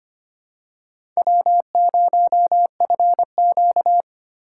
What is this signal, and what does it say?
Morse code, heard as a keyed tone, about 25 wpm, 700 Hz. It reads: W0FQ
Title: USA Call Signs - 4 Characters - 2x Character Spacing - 25wpm